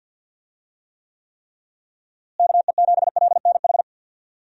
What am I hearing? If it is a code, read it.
Morse code, heard as a keyed tone, about 50 wpm, 700 Hz. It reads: KE7LNH
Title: USA Call Signs - 6 Characters - Repeated 3x - 50wpm